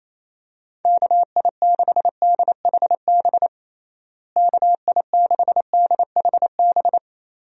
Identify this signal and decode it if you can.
Morse code, heard as a keyed tone, about 28 wpm, 700 Hz. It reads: KI6DHB KI6DHB
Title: USA Call Signs - 6 Characters - Repeated 3x - 28wpm